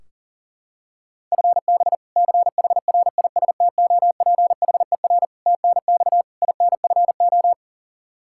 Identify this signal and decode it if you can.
Morse code, heard as a keyed tone, about 40 wpm, 700 Hz. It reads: FB CHRISTOPHER TNX INFO
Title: K1USN SST QSO Practice 40wpm